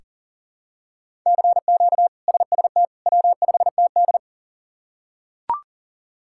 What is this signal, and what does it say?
Morse code, heard as a keyed tone, about 40 wpm, 700 Hz. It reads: CQ SST W5TD
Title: K1USN SST QSO Practice - 40wpm